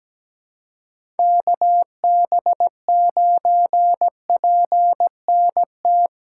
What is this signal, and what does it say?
Morse code, heard as a keyed tone, about 17 wpm, 700 Hz. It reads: KB9PNT
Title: USA Call Signs - 6 Characters - 3x Character Spacing - 17wpm